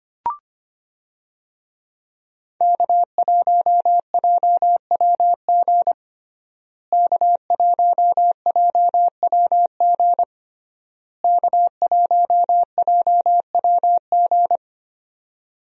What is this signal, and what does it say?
Morse code, heard as a keyed tone, about 25 wpm, 700 Hz. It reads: K1JWG K1JWG K1JWG
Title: USA Call Signs - 5 Characters - Repeated 3x - 25wpm